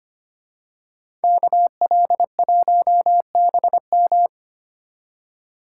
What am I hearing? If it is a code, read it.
Morse code, heard as a keyed tone, about 29 wpm, 700 Hz. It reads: KL1BM
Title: Speed Racing - USA 2x2 Call Signs 25wpm